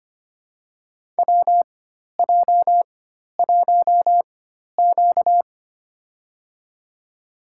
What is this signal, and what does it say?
Morse code, heard as a keyed tone, about 25 wpm, spaced as for 12 wpm, 700 Hz. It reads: WJ1Q
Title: USA Call Signs - 4x Character Spacing - 25wpm